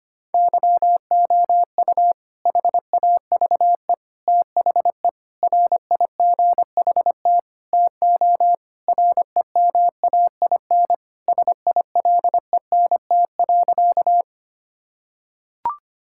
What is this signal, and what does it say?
Morse code, heard as a keyed tone, about 25 wpm, 700 Hz. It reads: YOU HAVE THE RIGHT TO REMAIN SILENT.
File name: practice / 25wpm / Sentences with 4-Word Phrases - 25wpm